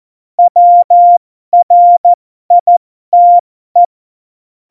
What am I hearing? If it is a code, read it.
Morse code, heard as a keyed tone, about 14 wpm, spaced as for 12 wpm, 700 Hz. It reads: WRITE
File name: Morsekurs / Witze